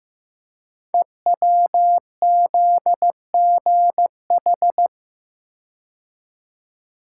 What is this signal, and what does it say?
Morse code, heard as a keyed tone, about 15 wpm, 700 Hz. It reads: EWZGH
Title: Code Groups - 5 Letters - 15wpm